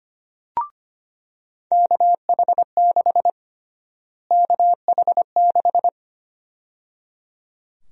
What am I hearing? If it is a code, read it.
Morse code, heard as a keyed tone, about 25 wpm, 700 Hz. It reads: KH6 KH6
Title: USA Call Sign Prefixes - Repeated 2x - 25wpm